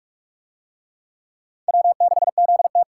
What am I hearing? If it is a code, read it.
Morse code, heard as a keyed tone, about 44 wpm, 700 Hz. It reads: W6ZT